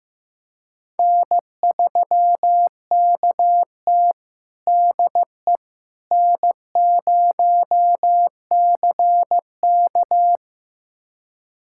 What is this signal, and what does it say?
Morse code, heard as a keyed tone, about 15 wpm, 700 Hz. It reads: N3KT DE N0CK
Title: USA Callsign DE Callsign - 4 Characters - 15wpm